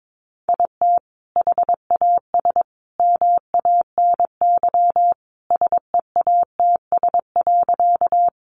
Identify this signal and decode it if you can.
Morse code, heard as a keyed tone, about 22 wpm, 700 Hz. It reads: IT HAS MANY SEATS.
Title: Additional Easy Phrases 22wpm